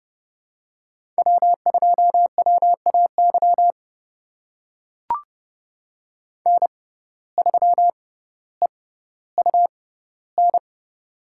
Morse code, heard as a keyed tone, about 30 wpm, spaced as for 15 wpm, 700 Hz. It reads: W2WAY N3EUN
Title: USA Call Signs - 5 Characters - 6x Character Spacing - 30wpm